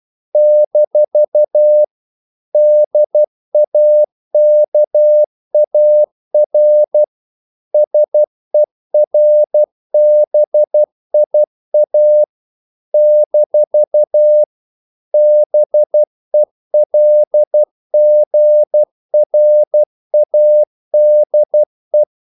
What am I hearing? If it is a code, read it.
Morse code, heard as a keyed tone, about 12 wpm, 600 Hz. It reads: - DAKAR SERBIA - BELGRADE
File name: CW / Practice / NationalCapitals12